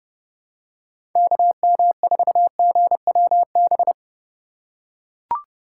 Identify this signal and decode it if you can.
Morse code, heard as a keyed tone, about 30 wpm, 700 Hz. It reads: KM4GWB
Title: USA Call Signs - 6 Characters - 3x Character Spacing - 30wpm